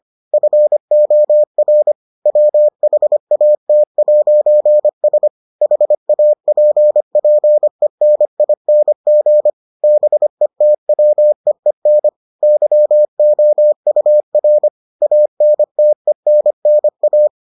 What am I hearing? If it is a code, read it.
Morse code, heard as a keyed tone, about 25 wpm, 600 Hz. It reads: FOR WHAT'S HAPPENING BETWEEN YOUR ANTENNA